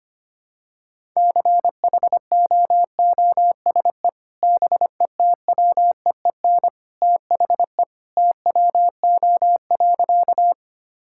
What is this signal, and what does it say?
Morse code, heard as a keyed tone, about 25 wpm, 700 Hz. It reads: CHOOSE BETWEEN THE TWO.
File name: practice / 25wpm / Sentences with 2-Word Phrases - 25wpm